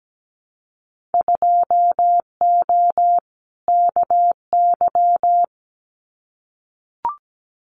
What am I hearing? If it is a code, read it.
Morse code, heard as a keyed tone, about 17 wpm, 700 Hz. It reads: 2O KY
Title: Winter Field Day Exchanges 17wpm